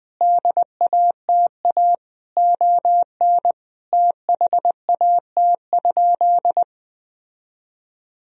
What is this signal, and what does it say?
Morse code, heard as a keyed tone, about 20 wpm, 700 Hz. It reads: DATA ON THAT?
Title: Sentences from Top 1000 Words - Review Part 3 - 20wpm